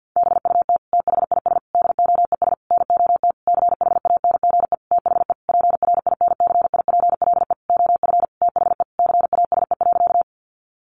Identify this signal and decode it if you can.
Morse code, heard as a keyed tone, about 50 wpm, 700 Hz. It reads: BUT THIS DOES NOT CHANGE THE PRINCIPLE OF THE CASE.
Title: Sentences from Top 1400 Words 50wpm